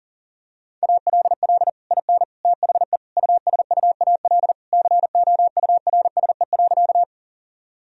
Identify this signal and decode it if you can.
Morse code, heard as a keyed tone, about 37 wpm, 700 Hz. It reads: ALL IN THE USUAL COURSE.
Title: Sentences from Top 1000 Words - Review Part 2 - 40wpm